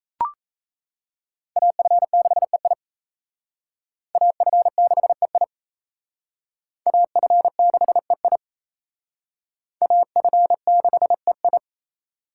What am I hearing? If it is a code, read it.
Morse code, heard as a keyed tone, about 34 wpm, 700 Hz. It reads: AF6EI AF6EI AF6EI AF6EI
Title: Speed Racing - USA 2x2 Call Signs 28wpm